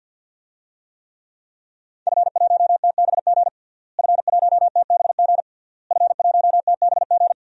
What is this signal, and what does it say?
Morse code, heard as a keyed tone, about 52 wpm, 700 Hz. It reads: F1TBG F1TBG F1TBG
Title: DXCC-100 Call Signs - 5 Characters - Repeated 3x - 50wpm